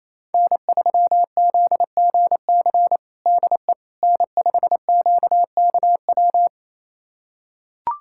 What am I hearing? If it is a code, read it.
Morse code, heard as a keyed tone, about 28 wpm, 700 Hz. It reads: N3ZGC DE N5QKW
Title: USA Callsign DE Callsign - 5 Characters - 28wpm